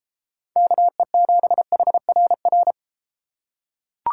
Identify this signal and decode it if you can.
Morse code, heard as a keyed tone, about 33 wpm, 700 Hz. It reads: KE7HRR